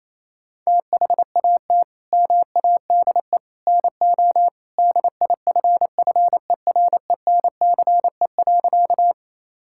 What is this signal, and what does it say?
Morse code, heard as a keyed tone, about 28 wpm, 700 Hz. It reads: THAT MADE NO DIFFERENCE.
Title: Sentences from Top 1000 Words - Review Part 5 - 28wpm